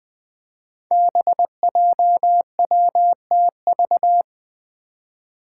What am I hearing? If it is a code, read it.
Morse code, heard as a keyed tone, about 21 wpm, 700 Hz. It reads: BJWTV
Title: Five Challenging Letter-Number Combinations 20wpm